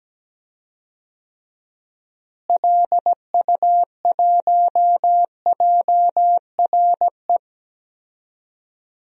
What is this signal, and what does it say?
Morse code, heard as a keyed tone, about 17 wpm, 700 Hz. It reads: LU1JRE
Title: DXCC-100 Call Signs - 6 Characters - Repeated 3x - 17wpm